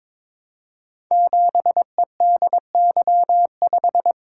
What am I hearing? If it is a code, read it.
Morse code, heard as a keyed tone, about 22 wpm, 700 Hz. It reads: 7EDY5
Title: Code Groups - 5 Letter:Numbers - 22wpm